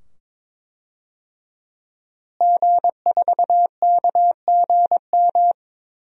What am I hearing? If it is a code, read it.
Morse code, heard as a keyed tone, about 22 wpm, 700 Hz. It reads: G4KGM
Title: DXCC-100 Call Signs - 5 Characters - Repeated 2x - 22wpm